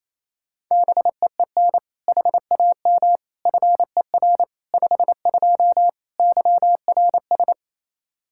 Morse code, heard as a keyed tone, about 28 wpm, 700 Hz. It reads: BEEN HAM FER 52 YRS
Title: Practice QSO - Exchange 3 - More Info - 28wpm